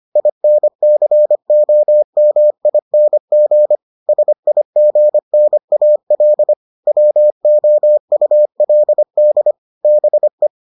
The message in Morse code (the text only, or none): INCOMING SIGNAL WOULD BE